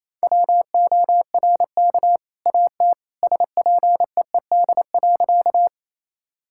Morse code, heard as a keyed tone, about 28 wpm, 700 Hz. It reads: WORK AT SPEED.
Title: Sentences from Top 1000 Words - Review Part 5 - 28wpm